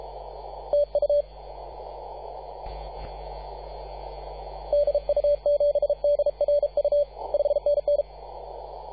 Morse code, heard as a keyed tone, about 33 wpm, 580 Hz. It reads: TU DU7DRU 5NN
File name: posts / 2015-sept-24-2118z-t2gc-18-071-cw